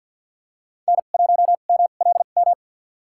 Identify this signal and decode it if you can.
Morse code, heard as a keyed tone, about 52 wpm, spaced as for 37 wpm, 700 Hz. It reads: N1MLM
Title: USA Call Signs - 2x Character Spacing - 50wpm